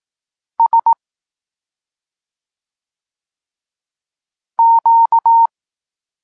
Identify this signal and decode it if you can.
Morse code, heard as a keyed tone, about 18 wpm, 920 Hz. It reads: S Q
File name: Tekst-34 4